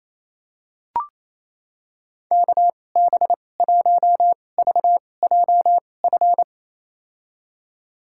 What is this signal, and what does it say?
Morse code, heard as a keyed tone, about 28 wpm, spaced as for 20 wpm, 700 Hz. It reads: KB1VJF
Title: USA Call Signs - 6 Characters - 2x Character Spacing - 28wpm